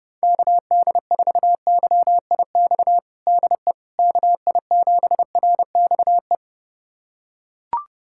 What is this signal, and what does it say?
Morse code, heard as a keyed tone, about 30 wpm, 700 Hz. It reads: KD4YIX DE KI7RXE
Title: USA Callsign DE Callsign - 6 Characters - 30wpm